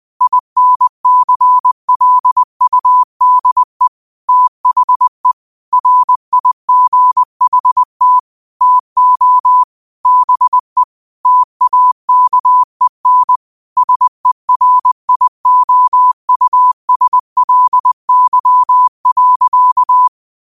Morse code, heard as a keyed tone, about 20 wpm, 1000 Hz. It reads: INCLUDE THE RIGHT TO BE TAKEN SERIOUSLY.
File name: qotdfiles / 20wpm